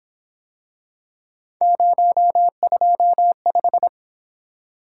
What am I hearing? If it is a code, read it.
Morse code, heard as a keyed tone, about 26 wpm, 700 Hz. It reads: 025
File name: Speed-Racing - Three Letter-Number Combinations - 22wpm